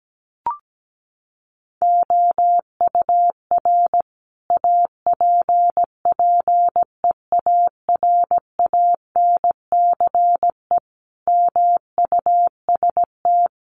Morse code, heard as a keyed tone, about 17 wpm, 700 Hz. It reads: OUR APPEARANCE MUST
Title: Sentences from Top 1100 Words 17wpm